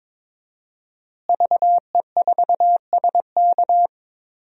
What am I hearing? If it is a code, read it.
Morse code, heard as a keyed tone, about 23 wpm, 700 Hz. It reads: VE4SK
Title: DXCC-100 Call Signs - 5 Characters - Repeated 2x - 22wpm